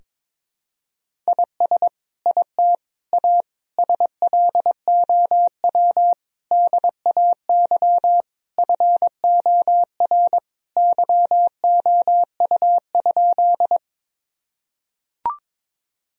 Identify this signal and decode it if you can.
Morse code, heard as a keyed tone, about 22 wpm, 700 Hz. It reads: IS IT A SLOW DAY FOR YOU?
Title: Sentences from Top 1000 Words - Review Part 2 - 22wpm